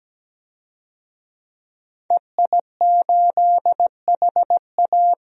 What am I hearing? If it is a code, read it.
Morse code, heard as a keyed tone, about 17 wpm, 700 Hz. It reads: EI8HA
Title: DXCC-100 Call Signs - 5 Characters - Repeated 3x - 17wpm